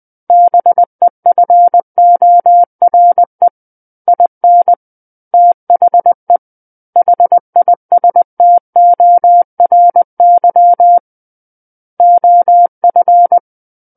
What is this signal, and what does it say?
Morse code, heard as a keyed tone, about 20 wpm, 700 Hz. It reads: BEFORE IN THE HISTORY OF
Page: SKCC Learning Center - Morse Code Practice Files